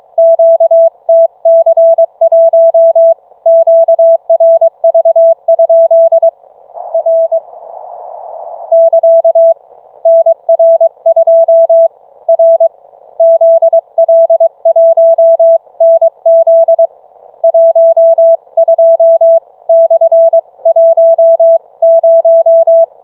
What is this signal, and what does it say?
Morse code, heard as a keyed tone, about 22 wpm, 660 Hz. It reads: QTC1 QRV? R <KA> NR2 R ZL1NZ 12/10